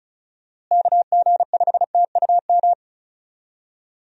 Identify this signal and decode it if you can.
Morse code, heard as a keyed tone, about 36 wpm, 700 Hz. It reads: KG5TUM
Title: USA Call Signs - 6 Characters - Repeated 3x - 35wpm